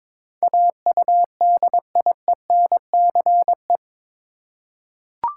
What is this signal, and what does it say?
Morse code, heard as a keyed tone, about 22 wpm, 700 Hz. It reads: AUDIENCE
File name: practice / 22wpm / Eight Letter Words - 22wpm